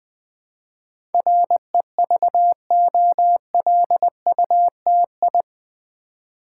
Morse code, heard as a keyed tone, about 20 wpm, 700 Hz. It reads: REVOLUTI
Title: Build A Word - 4 Letters At A Time - 10-Letter Words - 20wpm